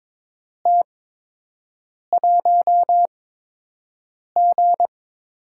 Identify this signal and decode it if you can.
Morse code, heard as a keyed tone, about 22 wpm, spaced as for 6 wpm, 700 Hz. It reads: T1G
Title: USA Call Signs - 4 Characters - 8x Character Spacing - 22wpm